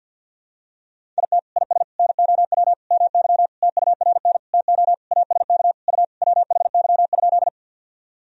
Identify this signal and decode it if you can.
Morse code, heard as a keyed tone, about 52 wpm, 700 Hz. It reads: IT IS NOW MY TURN TO ASK U WHY?
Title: Sentences from Top 200 Words and Abbreviations 50wpm